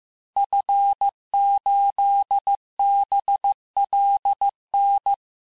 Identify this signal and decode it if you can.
Morse code, heard as a keyed tone, about 15 wpm, 800 Hz. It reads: F8BLN
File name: cours-du-lundi files / cours-1585-texte-FIN-de-QSO-vitesse-14mpm